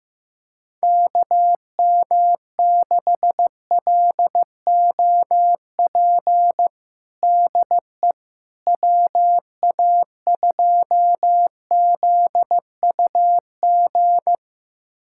Morse code, heard as a keyed tone, about 16 wpm, 700 Hz. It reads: KM6LOP DE WA2ZUG
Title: USA Callsign DE Callsign - 6 Characters - 15wpm